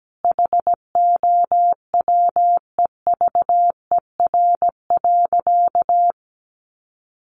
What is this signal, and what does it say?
Morse code, heard as a keyed tone, about 17 wpm, 700 Hz. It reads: HOWEVER.
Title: Sentences from Top 600 Words 17wpm